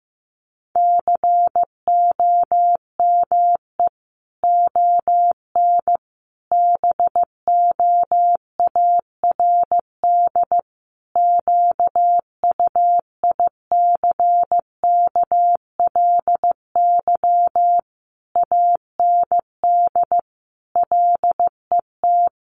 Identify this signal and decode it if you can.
Morse code, heard as a keyed tone, about 15 wpm, 700 Hz. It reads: COME ON BOARD QUICKLY AND LET
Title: Sentences from Top 1300 Words 15wpm